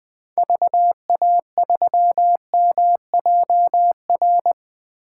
Morse code, heard as a keyed tone, about 20 wpm, 700 Hz. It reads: VA3MJR
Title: 2023 POTA Hunter Call Signs - 20wpm